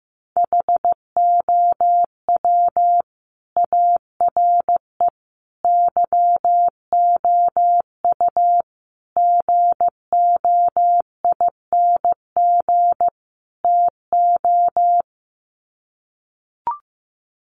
Morse code, heard as a keyed tone, about 15 wpm, 700 Hz. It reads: HOW ARE YOU GOING TO
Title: Sets of 5 Words 15wpm